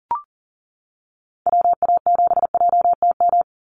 Morse code, heard as a keyed tone, about 40 wpm, 700 Hz. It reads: WA7JTM
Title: NA SOTA Activator Callsigns 40wpm